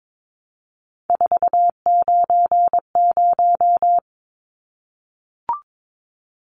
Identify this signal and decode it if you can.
Morse code, heard as a keyed tone, about 22 wpm, 700 Hz. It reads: 490
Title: Numbers - 3-Digit 22wpm